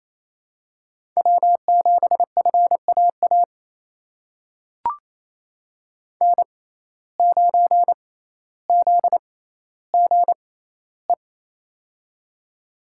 Morse code, heard as a keyed tone, about 28 wpm, spaced as for 14 wpm, 700 Hz. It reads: W7FAA N9ZGE
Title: USA Call Signs - 5 Characters - 6x Character Spacing - 28wpm